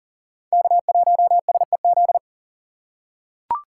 Morse code, heard as a keyed tone, about 40 wpm, 700 Hz. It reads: K1SEZ
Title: USA Call Signs - 5 Characters - 3x Character Spacing - 40wpm